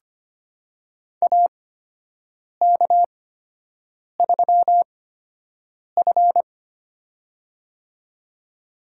Morse code, heard as a keyed tone, about 25 wpm, spaced as for 7 wpm, 700 Hz. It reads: AK3F
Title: USA Call Signs - 4 Characters - 8x Character Spacing - 25wpm